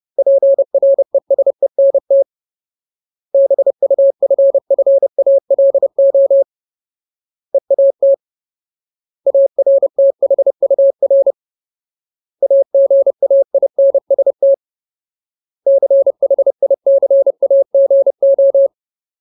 Morse code, heard as a keyed tone, about 30 wpm, 550 Hz. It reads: PRESENT BUFFALO EAT ARTHUR AGAINST CHICAGO
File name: Ultimate Word List / 30WPM Ultimate